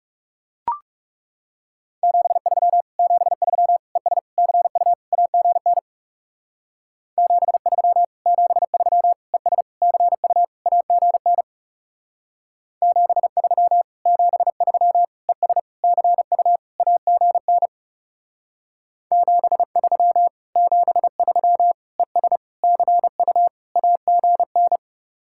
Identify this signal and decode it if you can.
Morse code, heard as a keyed tone, about 37 wpm, 700 Hz. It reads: 73 73 ES CU AGN 73 73 ES CU AGN 73 73 ES CU AGN 73 73 ES CU AGN
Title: Speed-Racing Common QSO Phrases - 30wpm